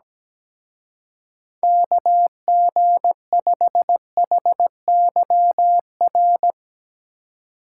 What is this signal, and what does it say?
Morse code, heard as a keyed tone, about 17 wpm, 700 Hz. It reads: KG5HYR
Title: Speed Racing - USA 2x3 Call Signs 17wpm